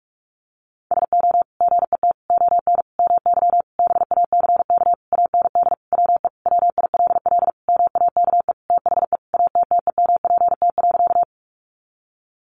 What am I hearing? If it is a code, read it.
Morse code, heard as a keyed tone, about 44 wpm, 700 Hz. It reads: SO GET ON MY BACK AND WE WILL MAKE THE ATTEMPT.
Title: Sentences from Top 1000 Words 45wpm